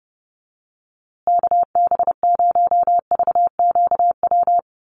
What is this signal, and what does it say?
Morse code, heard as a keyed tone, about 30 wpm, 700 Hz. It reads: KB0VQW
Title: USA Call Signs Repeated 2x 30wpm